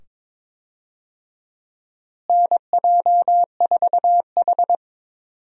Morse code, heard as a keyed tone, about 22 wpm, 700 Hz. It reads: NJ4H
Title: USA Call Signs - 4 Characters - Repeated 2x - 22wpm